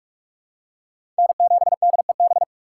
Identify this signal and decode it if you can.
Morse code, heard as a keyed tone, about 41 wpm, 700 Hz. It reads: N7DEB
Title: USA Call Signs - 5 Characters - 3x Character Spacing - 45wpm